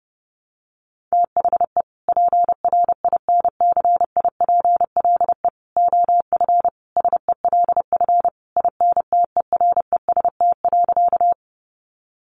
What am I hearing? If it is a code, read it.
Morse code, heard as a keyed tone, about 30 wpm, 700 Hz. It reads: THE PRINCIPLE OF SELF INTEREST.
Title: Sentences from Top 1400 Words 30wpm